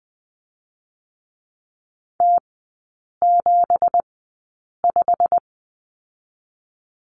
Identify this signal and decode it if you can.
Morse code, heard as a keyed tone, about 20 wpm, spaced as for 8 wpm, 700 Hz. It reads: T75
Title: 166 Lesson 27e - Number 7 - Three Characters 20wpm